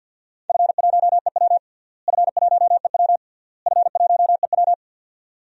Morse code, heard as a keyed tone, about 50 wpm, 700 Hz. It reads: F1EW F1EW F1EW
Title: DXCC-100 Call Signs - 4 Characters - Repeated 3x - 50wpm